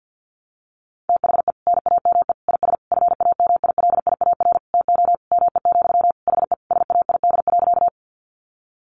Morse code, heard as a keyed tone, about 48 wpm, 700 Hz. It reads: THE NAME IS FAMILIAR TO ME, HE SAID.
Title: Sentences from Top 1000 Words 50wpm